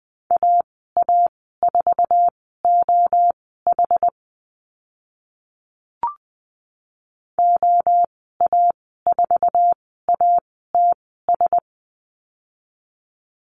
Morse code, heard as a keyed tone, about 20 wpm, spaced as for 15 wpm, 700 Hz. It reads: AA4OH OA4ATS
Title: 076 Lesson 13g - Letter L - Call Signs 20wpm